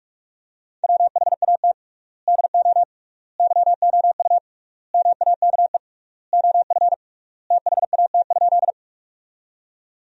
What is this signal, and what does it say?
Morse code, heard as a keyed tone, about 46 wpm, 690 Hz. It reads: WHAT DO YOU MAKE OF THAT?
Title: 3x Word-Spacing - Sentences from Top 100 Words - 45wpm